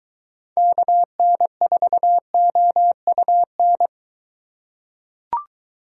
Morse code, heard as a keyed tone, about 22 wpm, 700 Hz. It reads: KN4OUN